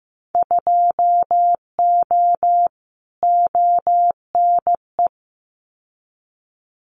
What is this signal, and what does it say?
Morse code, heard as a keyed tone, about 15 wpm, 700 Hz. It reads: 2O ONE
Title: Winter Field Day Exchanges Encore 15wpm